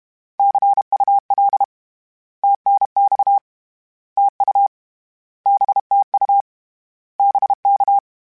Morse code, heard as a keyed tone, about 31 wpm, 800 Hz. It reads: CUL TNX TU BTU BK
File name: Misc / Misc-mp3s / Words @30